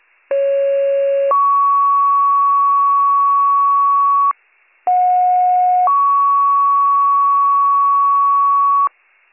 Two-tone page, heard as a one-second tone, then a three-second tone, about 550 then 1120 Hz, about 730 then 1120 Hz. It is heard from 0.3 to 8.9 s.